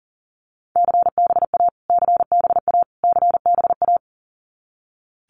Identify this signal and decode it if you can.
Morse code, heard as a keyed tone, about 40 wpm, 700 Hz. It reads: CBA CBA CBA
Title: Abbreviations Repeated 3x - 40wpm